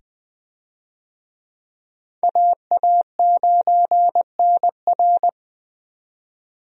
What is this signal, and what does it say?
Morse code, heard as a keyed tone, about 20 wpm, 700 Hz. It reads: AA9NR
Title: USA Call Signs - 5 Characters - 20wpm